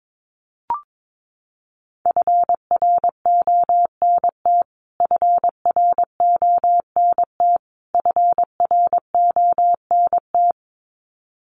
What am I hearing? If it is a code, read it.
Morse code, heard as a keyed tone, about 22 wpm, 700 Hz. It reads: FRONT FRONT FRONT
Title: Top 500 Words Repeated 22wpm